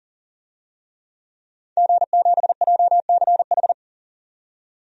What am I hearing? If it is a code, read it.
Morse code, heard as a keyed tone, about 40 wpm, 700 Hz. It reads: G7JCH
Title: DXCC-100 Call Signs - 5 Characters - 40wpm